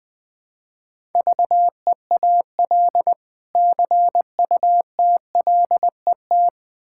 Morse code, heard as a keyed tone, about 20 wpm, 700 Hz. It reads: VEAL CUTLET
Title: Difficult Words with Hints - 20wpm